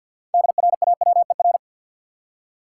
Morse code, heard as a keyed tone, about 52 wpm, 700 Hz. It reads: DRAWER
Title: Six Letter Words - 50wpm